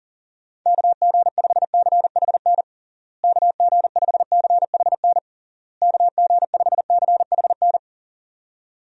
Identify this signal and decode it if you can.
Morse code, heard as a keyed tone, about 37 wpm, 700 Hz. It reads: KG5CHN KG5CHN KG5CHN
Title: USA Call Signs - 6 Characters - Repeated 3x - 40wpm